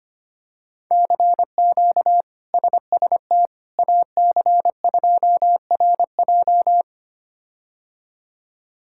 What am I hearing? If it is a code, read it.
Morse code, heard as a keyed tone, about 25 wpm, 700 Hz. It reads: CQ SST AC2RJ
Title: K1USN SST QSO Practice - 25wpm